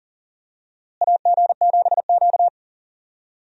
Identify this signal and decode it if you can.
Morse code, heard as a keyed tone, about 40 wpm, 700 Hz. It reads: AG7Q
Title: USA Call Signs - 4 Characters - Repeated 2x - 40wpm